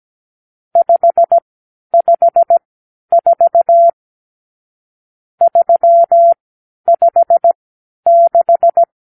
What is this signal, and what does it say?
Morse code, heard as a keyed tone, about 17 wpm, spaced as for 10 wpm, 680 Hz. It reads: 554 356
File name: k7qo / Lesson 50 (1-6)